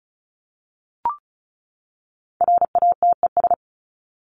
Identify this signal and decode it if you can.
Morse code, heard as a keyed tone, about 34 wpm, 700 Hz. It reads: RATES
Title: ES:S Suffix - Top 100 Words with Base Words 35wpm